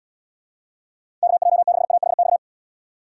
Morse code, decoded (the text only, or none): KJ6AHX